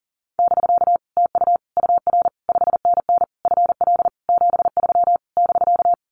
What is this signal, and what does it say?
Morse code, heard as a keyed tone, about 40 wpm, 700 Hz. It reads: <BK> TU UR 5NN FL 73 <BK>
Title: Common QSO Phrases Encore 40wpm